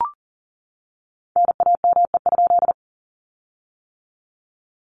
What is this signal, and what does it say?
Morse code, heard as a keyed tone, about 38 wpm, 700 Hz. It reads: NAME?
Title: Common QSO Phrases Encore 40wpm